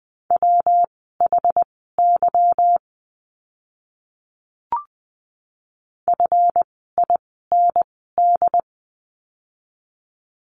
Morse code, heard as a keyed tone, about 20 wpm, spaced as for 15 wpm, 700 Hz. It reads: WHY FIND
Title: 146 Lesson 24f - Letter y - Words 20wpm